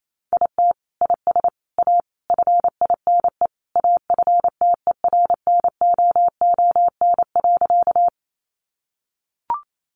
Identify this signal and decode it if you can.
Morse code, heard as a keyed tone, about 28 wpm, 700 Hz. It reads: IT IS A FINE AFTERNOON.
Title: Sentences from Top 1500 Words 28wpm